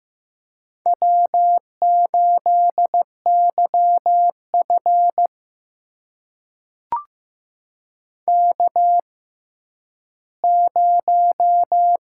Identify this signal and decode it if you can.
Morse code, heard as a keyed tone, about 15 wpm, spaced as for 10 wpm, 700 Hz. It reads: W8YF K0
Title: USA Call Signs - 4 Characters - 6x Character Spacing - 15wpm